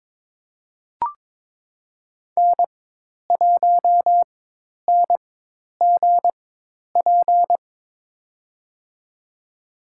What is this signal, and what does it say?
Morse code, heard as a keyed tone, about 23 wpm, spaced as for 10 wpm, 700 Hz. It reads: N1NGP
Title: USA Call Signs - 5 Characters - 4x Character Spacing - 22wpm